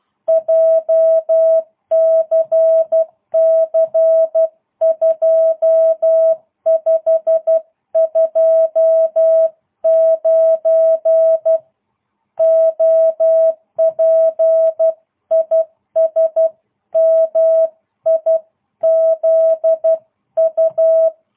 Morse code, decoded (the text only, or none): JCC 2529 OP IS MIZU